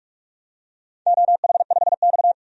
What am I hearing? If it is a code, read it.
Morse code, heard as a keyed tone, about 46 wpm, 700 Hz. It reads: OH5X
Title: DXCC-100 Call Signs - 4 Characters - 45wpm